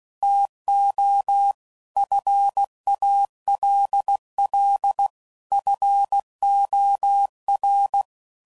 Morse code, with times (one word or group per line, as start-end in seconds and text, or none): 0.2-1.5 TO
2.0-5.1 FALL
5.5-8.0 FOR